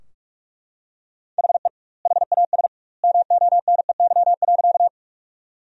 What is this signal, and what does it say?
Morse code, heard as a keyed tone, about 41 wpm, 700 Hz. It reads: HE HAS MONEY.
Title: 2x Word-Spacing - Sentences from Top 400 Words - 45wpm